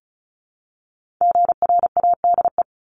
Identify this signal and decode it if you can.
Morse code, heard as a keyed tone, about 36 wpm, 700 Hz. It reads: GRADE
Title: Multi-letter Phonemes 35wpm